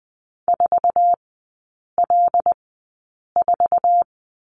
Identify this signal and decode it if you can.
Morse code, heard as a keyed tone, about 20 wpm, spaced as for 8 wpm, 700 Hz. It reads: 4L4
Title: 074 Lesson 13e - Letter L - Three Characters 20wpm